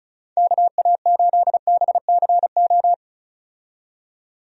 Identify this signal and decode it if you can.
Morse code, heard as a keyed tone, about 34 wpm, 700 Hz. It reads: KA8BCO
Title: USA Call Signs - 6 Characters - 35wpm